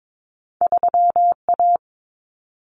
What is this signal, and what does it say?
Morse code, heard as a keyed tone, about 22 wpm, 700 Hz. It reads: 3A
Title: Common Cut Numbers - 2-Digit 22wpm